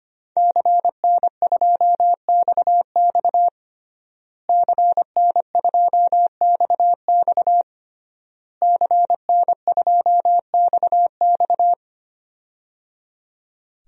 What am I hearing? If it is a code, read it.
Morse code, heard as a keyed tone, about 25 wpm, 700 Hz. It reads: CN2XX CN2XX CN2XX
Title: DXCC-100 Call Signs - 5 Characters - Repeated 3x - 25wpm